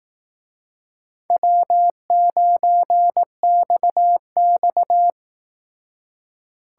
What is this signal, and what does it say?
Morse code, heard as a keyed tone, about 18 wpm, 700 Hz. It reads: W9XX
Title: Speed Racing - USA 1x2 Call Signs 15wpm